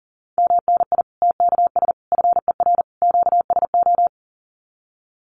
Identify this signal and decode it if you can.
Morse code, heard as a keyed tone, about 40 wpm, 700 Hz. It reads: MNI TKS FER QSO
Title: Common QSO Phrases 40wpm